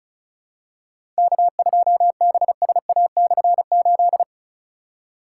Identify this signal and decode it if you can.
Morse code, heard as a keyed tone, about 34 wpm, 700 Hz. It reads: K2BSA/8
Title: K1USN SST QSO Practice - 35wpm